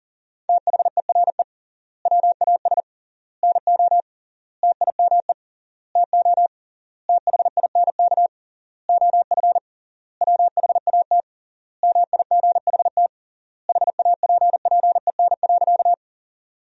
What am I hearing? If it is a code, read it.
Morse code, heard as a keyed tone, about 40 wpm, 700 Hz. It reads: THERE WAS NO TIME TO THINK OF WHAT MIGHT HAPPEN.
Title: 3x Word-Spacing - Sentences from Top 400 Words - 40wpm